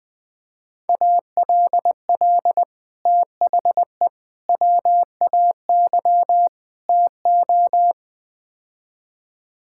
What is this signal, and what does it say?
Morse code, heard as a keyed tone, about 20 wpm, 700 Hz. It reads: ALL THE WAY TO
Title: Build A Phrase - 4-Word Phrases - 20wpm